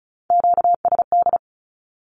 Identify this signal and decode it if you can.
Morse code, heard as a keyed tone, about 36 wpm, 700 Hz. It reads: QSD
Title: Q-Codes with Spoken Meaning - 35wpm